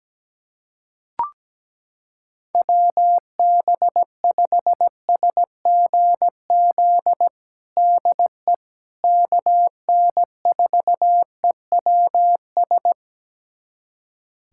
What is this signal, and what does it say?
Morse code, heard as a keyed tone, about 17 wpm, 700 Hz. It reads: WB5SGZ DE KN4EWS
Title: USA Callsign DE Callsign 17wpm